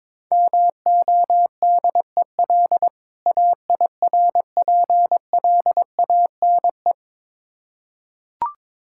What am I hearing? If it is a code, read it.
Morse code, heard as a keyed tone, about 22 wpm, 690 Hz. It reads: MODEL AIRPLANE
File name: Difficult Words with Hints - 22wpm